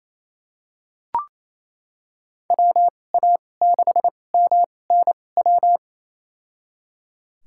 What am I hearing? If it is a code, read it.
Morse code, heard as a keyed tone, about 27 wpm, spaced as for 20 wpm, 700 Hz. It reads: WA6MNW
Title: USA Call Signs - 6 Characters - 2x Character Spacing - 28wpm